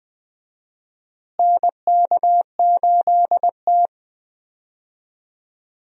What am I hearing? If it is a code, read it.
Morse code, heard as a keyed tone, about 20 wpm, 700 Hz. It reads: NK8T
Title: USA Call Signs - 4 Characters - Repeated 3x - 20wpm